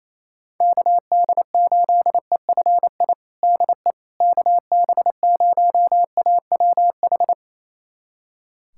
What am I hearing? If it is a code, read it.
Morse code, heard as a keyed tone, about 28 wpm, 700 Hz. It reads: KD8EFI DE KB0AWH
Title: USA Callsign DE Callsign - 6 Characters - 28wpm